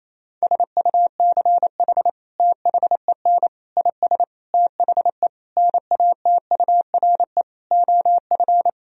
Morse code, heard as a keyed tone, about 28 wpm, 700 Hz. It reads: SUCH THEN IS THE NATURE OF
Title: Sentences from Top 1000 Words - Review Part 4 - 28wpm